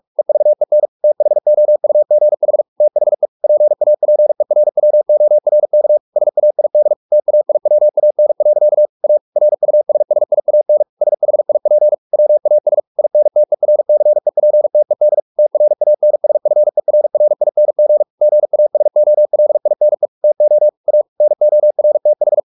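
Morse code, heard as a keyed tone, about 44 wpm, 600 Hz. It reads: EVEN THOUGH THE PAPERWORK SAID TAIWAN. A RUSSIAN SHIP WAS INTERCEPTED TRANSFERRING GASOLINE TO A NORTH
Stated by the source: Morse practice at 45 wpm, the novel Never by Ken Follett in English, from CW med Gnister.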